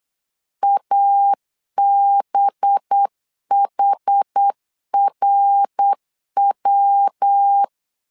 Morse code, heard as a keyed tone, about 8 wpm, 800 Hz. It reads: ABHRW
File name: LEZIONI / 6 PKX0 / 6 PKX0 8wpm 3341